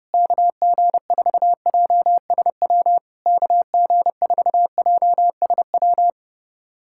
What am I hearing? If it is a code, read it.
Morse code, heard as a keyed tone, about 30 wpm, 700 Hz. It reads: KG4JSW KG4JSW
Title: SOTA QSO Practice 30wpm